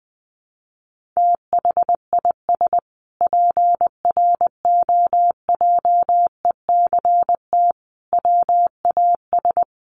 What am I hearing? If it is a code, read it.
Morse code, heard as a keyed tone, about 20 wpm, 700 Hz. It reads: THIS PROJECT WAS
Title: Sentences from Top 1400 Words 20wpm